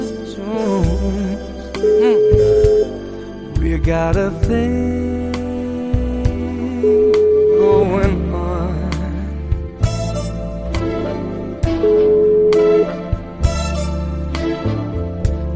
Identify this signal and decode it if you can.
Ringback tone (Europe, 425 Hz), with noise.